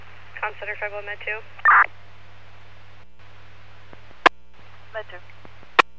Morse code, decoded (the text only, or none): none